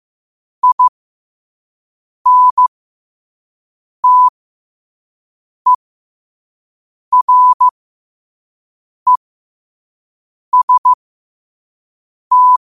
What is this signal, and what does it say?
Morse code, heard as a keyed tone, about 15 wpm, spaced as for 5 wpm, 1000 Hz. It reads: INTEREST
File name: qotdfiles / 5wpm